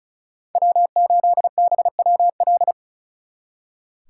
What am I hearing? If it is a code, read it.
Morse code, heard as a keyed tone, about 47 wpm, spaced as for 41 wpm, 700 Hz. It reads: W8BWL
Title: Speed Racing - USA 1x3 Call Signs 35wpm